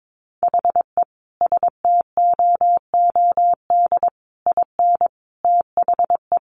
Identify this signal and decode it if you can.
Morse code, heard as a keyed tone, about 22 wpm, 700 Hz. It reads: HE STOOD IN THE
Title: Sentences from Top 1400 Words 22wpm